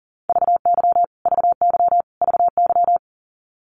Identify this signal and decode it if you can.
Morse code, heard as a keyed tone, about 40 wpm, 700 Hz. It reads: VY VY VY
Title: Abbreviations Repeated 3x - 40wpm